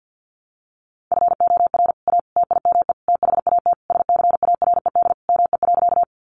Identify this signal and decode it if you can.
Morse code, heard as a keyed tone, about 50 wpm, 700 Hz. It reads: FOR A TIME THAT SCARED ME.
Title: Sentences from Top 700 Words 50wpm